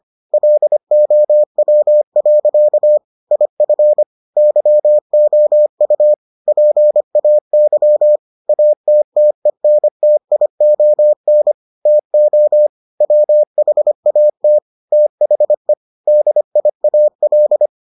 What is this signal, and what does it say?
Morse code, heard as a keyed tone, about 25 wpm, 600 Hz. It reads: LOW. IF YOU PAY ATTENTION TO WHAT THE DIAL